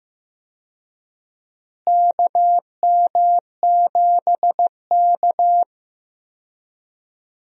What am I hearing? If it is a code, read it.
Morse code, heard as a keyed tone, about 15 wpm, 700 Hz. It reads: KM7K
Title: USA Call Signs - 4 Characters - Repeated 3x - 15wpm